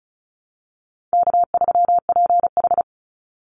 Morse code, heard as a keyed tone, about 34 wpm, 700 Hz. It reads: K3PH
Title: Call Signs Repeated 3x - 35wpm